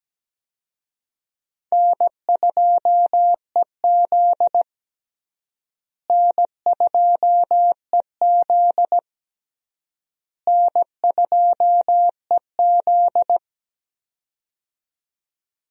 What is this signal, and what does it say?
Morse code, heard as a keyed tone, about 17 wpm, 700 Hz. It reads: N2EZ N2EZ N2EZ
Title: USA Call Signs - 4 Characters - Repeated 3x - 17wpm